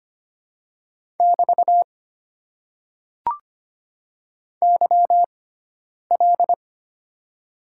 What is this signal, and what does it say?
Morse code, heard as a keyed tone, about 25 wpm, spaced as for 9 wpm, 700 Hz. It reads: <BT> YL
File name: Top 90 QSO Elements - 6x Character Spacing - 25wpm